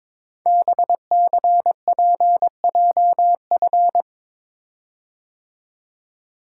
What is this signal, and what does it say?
Morse code, heard as a keyed tone, about 22 wpm, 700 Hz. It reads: BCPJF
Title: Code Groups - 5 Letters - 22wpm